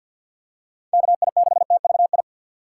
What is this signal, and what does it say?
Morse code, heard as a keyed tone, about 46 wpm, 700 Hz. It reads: KI6TVI
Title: USA Call Signs - 6 Characters - 3x Character Spacing - 50wpm